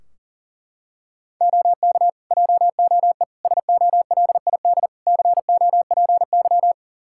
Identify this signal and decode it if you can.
Morse code, heard as a keyed tone, about 40 wpm, 700 Hz. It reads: OK JOE SOLID COPY
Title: Practice QSO - Exchange 3 - More Info - 40wpm